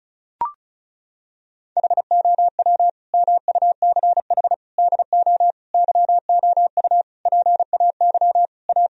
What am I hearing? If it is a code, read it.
Morse code, heard as a keyed tone, about 34 wpm, 700 Hz. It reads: HOW MUCH DO YOU PAY A
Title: Sentences from Top 1000 Words - Review Part 2 - 35wpm